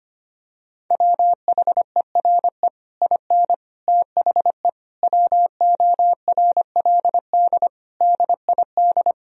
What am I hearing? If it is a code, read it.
Morse code, heard as a keyed tone, about 25 wpm, 700 Hz. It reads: WHERE IN THE WORLD DID
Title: Sentences from Top 200 Words and Abbreviations 25wpm